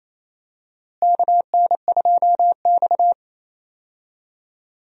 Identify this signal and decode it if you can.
Morse code, heard as a keyed tone, about 28 wpm, 700 Hz. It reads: KN2X
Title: USA Call Signs - 4 Characters - Repeated 3x - 28wpm